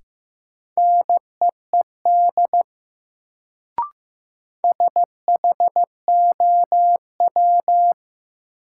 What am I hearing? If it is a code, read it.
Morse code, heard as a keyed tone, about 15 wpm, 700 Hz. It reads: NEED SHOW
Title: Sentences from Top 200 Words - New Word Drill - 15wpm